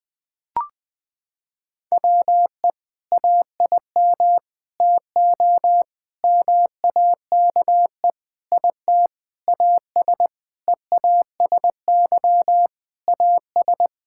Morse code, heard as a keyed tone, about 20 wpm, 700 Hz. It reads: WE AIM TO MAKE IT AS EASY AS
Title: Sentences with 7-Word Phrases - 20wpm